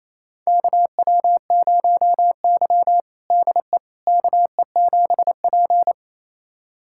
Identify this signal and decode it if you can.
Morse code, heard as a keyed tone, about 28 wpm, 700 Hz. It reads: KW0Y DE KE7P
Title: USA Callsign DE Callsign - 4 Characters - 28wpm